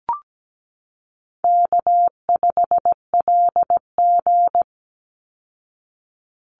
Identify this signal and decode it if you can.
Morse code, heard as a keyed tone, about 17 wpm, 700 Hz. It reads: K5LG
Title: Call Signs Repeated 3x - 17wpm